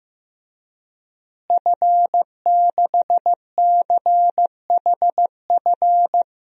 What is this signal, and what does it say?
Morse code, heard as a keyed tone, about 15 wpm, 700 Hz. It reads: F6CHF
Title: DXCC-100 Call Signs - 5 Characters - Repeated 3x - 15wpm